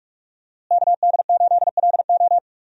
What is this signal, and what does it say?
Morse code, heard as a keyed tone, about 46 wpm, 700 Hz. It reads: KD8LO
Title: USA Call Signs - 5 Characters - Repeated 3x - 45wpm